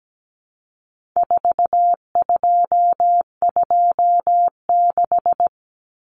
Morse code, heard as a keyed tone, about 17 wpm, 700 Hz. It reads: 4226
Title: Common Cut Numbers - 4-Digit 17wpm